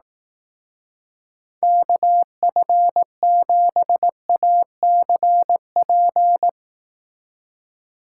Morse code, heard as a keyed tone, about 18 wpm, 700 Hz. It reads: KF7ACP
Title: Speed Racing - USA 2x3 Call Signs 15wpm